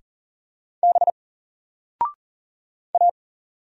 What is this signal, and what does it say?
Morse code, heard as a keyed tone, about 40 wpm, 700 Hz. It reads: B A
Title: Single Character (Letter:Number:Punctuation:Prosign) 40wpm